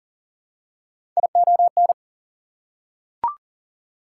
Morse code, heard as a keyed tone, about 37 wpm, 700 Hz. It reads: ION
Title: Prefixes and Suffixes 40wpm